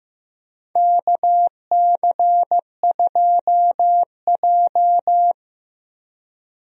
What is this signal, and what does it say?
Morse code, heard as a keyed tone, about 15 wpm, 700 Hz. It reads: KC2J
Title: USA Call Signs - 4 Characters - 15wpm